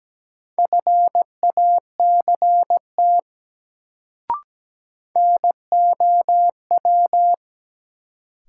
Morse code, heard as a keyed tone, about 17 wpm, 700 Hz. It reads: FACT NOW
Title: Sets of 2 Words - New Word Drill - 17wpm